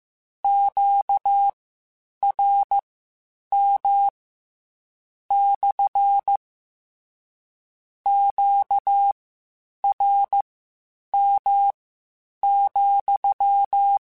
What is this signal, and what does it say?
Morse code, heard as a keyed tone, about 15 wpm, spaced as for 8 wpm, 800 Hz. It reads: QRM / QRM,
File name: Q Calls / Q Calls-mp3s / Q CALLS SPSL @13